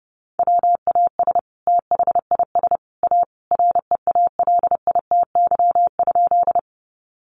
Morse code, heard as a keyed tone, about 30 wpm, 700 Hz. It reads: WAS THIS A REALITY?
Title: Sentences from Top 1200 Words 30wpm